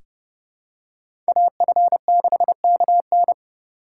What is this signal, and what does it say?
Morse code, heard as a keyed tone, about 30 wpm, 700 Hz. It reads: AF6KN
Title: USA Call Signs - 5 Characters - 4x Character Spacing - 30wpm